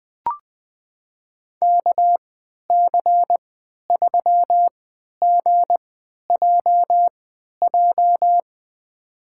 Morse code, heard as a keyed tone, about 20 wpm, spaced as for 11 wpm, 700 Hz. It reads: KC3GJJ
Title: USA Call Signs - 6 Characters - 3x Character Spacing - 20wpm